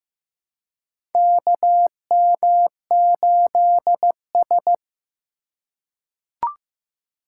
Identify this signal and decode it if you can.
Morse code, heard as a keyed tone, about 15 wpm, 700 Hz. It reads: KM8S
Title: USA Call Signs - 4 Characters - 4x Character Spacing - 15wpm